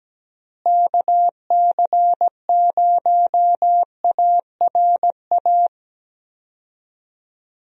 Morse code, heard as a keyed tone, about 17 wpm, 700 Hz. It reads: KC0ARA
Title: USA Call Signs - 6 Characters - Repeated 2x - 17wpm